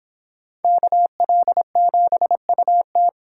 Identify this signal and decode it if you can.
Morse code, heard as a keyed tone, about 25 wpm, 700 Hz. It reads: KL7UT